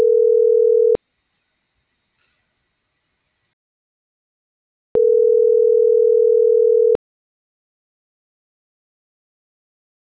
Ringback tone (North America), 440 + 480 Hz.